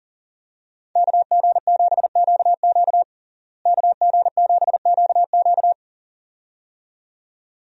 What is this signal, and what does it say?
Morse code, heard as a keyed tone, about 40 wpm, 700 Hz. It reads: KG7QQ KG7QQ
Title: USA Call Signs - 5 Characters - Repeated 2x - 40wpm